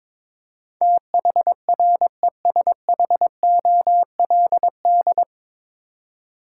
Morse code, heard as a keyed tone, about 21 wpm, 700 Hz. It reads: THRESHOLD
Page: Morse Code Ninja practice 22wpm